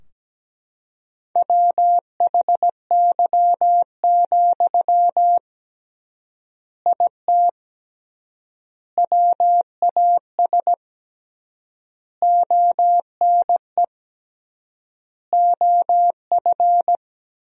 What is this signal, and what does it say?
Morse code, heard as a keyed tone, about 17 wpm, 700 Hz. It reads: WHY, IT WAS ONE OF
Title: 3x Word-Spacing - Sentences from Top 300 Words - 17wpm